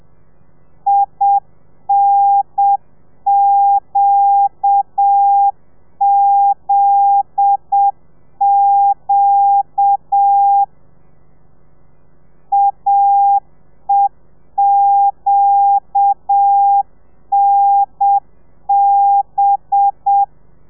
Morse code, heard as a keyed tone, about 7 wpm, 800 Hz. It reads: INQZQ AEQNB